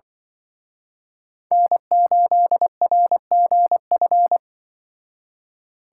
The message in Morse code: N8RGF

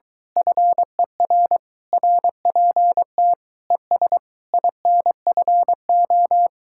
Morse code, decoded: FER RPT ES INFO

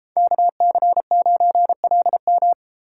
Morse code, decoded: KC9LM

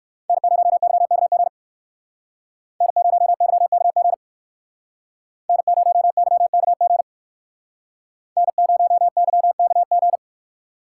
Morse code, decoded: N0YKG N0YKG N0YKG N0YKG

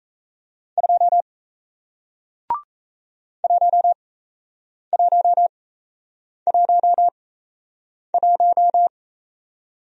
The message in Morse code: 2 1111

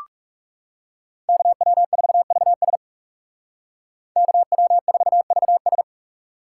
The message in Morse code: KW4VS KW4VS

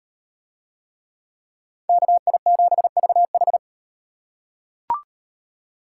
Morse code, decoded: KI7VH

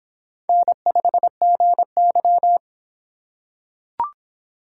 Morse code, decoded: N5GY

